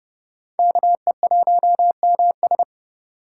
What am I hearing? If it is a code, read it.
Morse code, heard as a keyed tone, about 28 wpm, 700 Hz. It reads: KE1MS